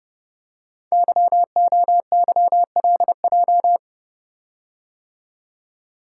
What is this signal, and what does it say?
Morse code, heard as a keyed tone, about 28 wpm, 700 Hz. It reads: YOYLJ
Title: Code Groups - 5 Letter:Numbers - 30wpm